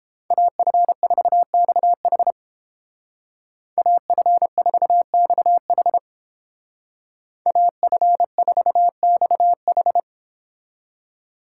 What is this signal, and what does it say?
Morse code, heard as a keyed tone, about 31 wpm, 700 Hz. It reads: AF4XH AF4XH AF4XH